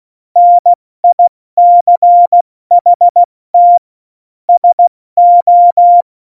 Morse code, heard as a keyed tone, about 16 wpm, 700 Hz. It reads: NICHT SO
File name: Morsekurs / Witze / Witz 15. September 2020, 70 F80